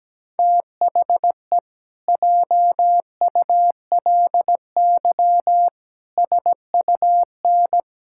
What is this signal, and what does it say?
Morse code, heard as a keyed tone, about 17 wpm, 700 Hz. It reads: THE JULY SUN
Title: Pangram Challenge v2 - 17wpm